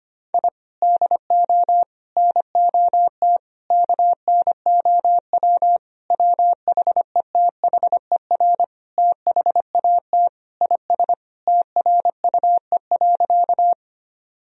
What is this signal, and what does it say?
Morse code, heard as a keyed tone, about 25 wpm, 700 Hz. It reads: I DO NOT KNOW WHETHER THAT IS TRUE.
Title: Sentences from Top 1000 Words - Review Part 3 - 25wpm